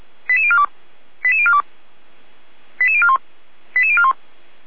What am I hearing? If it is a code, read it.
Five-tone selective call (ZVEI), heard as quick runs of five tones, tones of about 70 ms. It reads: 88042 88042 88041 88041